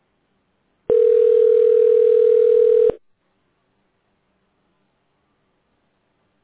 Ringback tone (North America), 440 + 480 Hz.